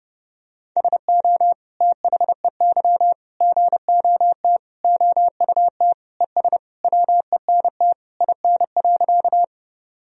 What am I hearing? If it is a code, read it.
Morse code, heard as a keyed tone, about 32 wpm, 700 Hz. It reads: SO THEY GOT OUT ES WENT IN.
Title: Sentences from Top 300 Words and Abbreviations 30wpm